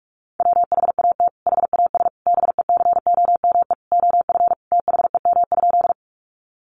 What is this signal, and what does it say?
Morse code, heard as a keyed tone, about 44 wpm, 700 Hz. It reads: WHAT HAS BECOME OF THEM?
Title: Sentences from Top 600 Words 45wpm